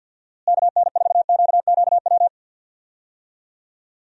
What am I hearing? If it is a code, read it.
Morse code, heard as a keyed tone, about 46 wpm, 700 Hz. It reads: KN4QQW
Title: USA Call Signs - 6 Characters - Repeated 2x - 50wpm